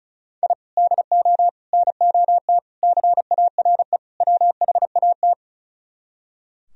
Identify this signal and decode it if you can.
Morse code, heard as a keyed tone, about 44 wpm, 700 Hz. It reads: I DO NOT CARE WHAT